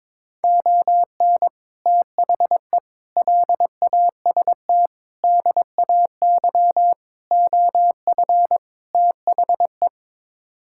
Morse code, heard as a keyed tone, about 22 wpm, 700 Hz. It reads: ON THE LAST DAY OF THE
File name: practice / 22wpm / Sets of 6 Words - 22wpm